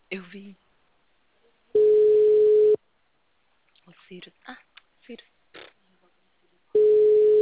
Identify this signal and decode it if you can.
Ringback tone (Europe, 425 Hz), with speech.
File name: out- -2011- - - .131